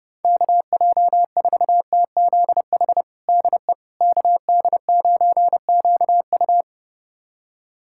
Morse code, heard as a keyed tone, about 32 wpm, 700 Hz. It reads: KJ4TZH DE KD9QU
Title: USA Callsign DE Callsign 30wpm